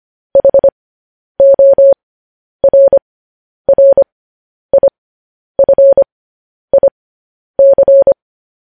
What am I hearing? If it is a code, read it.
Morse code, heard as a keyed tone, about 25 wpm, spaced as for 10 wpm, 550 Hz. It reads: HORRIFIC